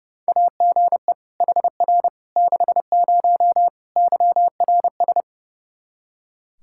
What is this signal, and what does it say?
Morse code, heard as a keyed tone, about 30 wpm, 700 Hz. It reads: AGE HR 60 YRS
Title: Practice QSO - Exchange 3 - More Info - 30wpm